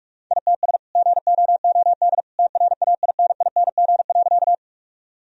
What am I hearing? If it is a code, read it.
Morse code, heard as a keyed tone, about 46 wpm, 700 Hz. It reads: ITS GOOD TRAINING.
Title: Sentences from Top 1000 Words - Review Part 4 - 45wpm